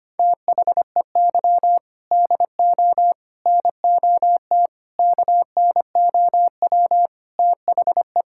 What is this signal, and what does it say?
Morse code, heard as a keyed tone, about 25 wpm, 700 Hz. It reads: THEY DO NOT KNOW THE
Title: Sentences from Top 1000 Words - Review Part 6 - 25wpm